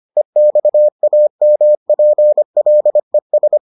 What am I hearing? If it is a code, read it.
Morse code, heard as a keyed tone, about 25 wpm, 600 Hz. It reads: EXAMPLES